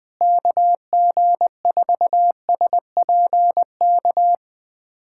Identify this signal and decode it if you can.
Morse code, heard as a keyed tone, about 21 wpm, 700 Hz. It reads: KG4SPK
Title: USA Callsign DE Callsign 20wpm